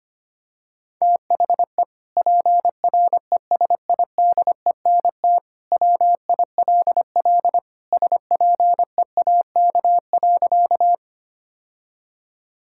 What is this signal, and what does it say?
Morse code, heard as a keyed tone, about 25 wpm, 700 Hz. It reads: THE PRESIDENT WILL SPEAK.
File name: practice / 25wpm / Sentences with 2-Word Phrases - 25wpm